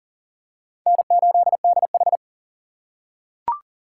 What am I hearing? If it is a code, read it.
Morse code, heard as a keyed tone, about 40 wpm, 700 Hz. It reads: N8DH
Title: USA Call Signs - 4 Characters - 4x Character Spacing - 40wpm